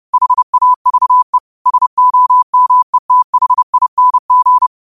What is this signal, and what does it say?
Morse code, heard as a keyed tone, about 30 wpm, 1000 Hz. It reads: HAVE SOMETHING
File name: qotdfiles / 30wpm